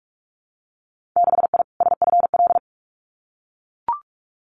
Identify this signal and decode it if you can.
Morse code, heard as a keyed tone, about 44 wpm, 700 Hz. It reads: 6I SFL
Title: Winter Field Day Exchanges 45wpm